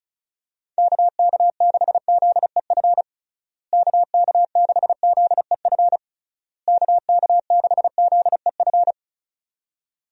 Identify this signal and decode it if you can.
Morse code, heard as a keyed tone, about 34 wpm, 700 Hz. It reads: KK6ZEF KK6ZEF KK6ZEF
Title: USA Call Signs - 6 Characters - Repeated 3x - 35wpm